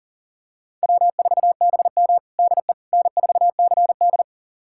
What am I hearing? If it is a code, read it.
Morse code, heard as a keyed tone, about 40 wpm, 700 Hz. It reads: W4BM DE N4CD
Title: USA Callsign DE Callsign - 4 Characters - 40wpm